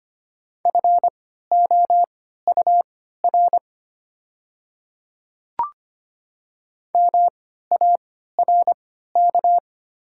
Morse code, heard as a keyed tone, about 25 wpm, spaced as for 14 wpm, 690 Hz. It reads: FOUR MARK
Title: 3x Character Spacing - Top 300 Words - 25wpm